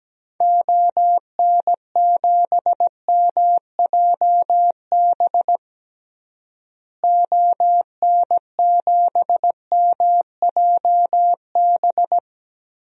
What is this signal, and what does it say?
Morse code, heard as a keyed tone, about 17 wpm, 700 Hz. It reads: ON7MJB ON7MJB
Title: DXCC-100 Call Signs - 6 Characters - Repeated 3x - 17wpm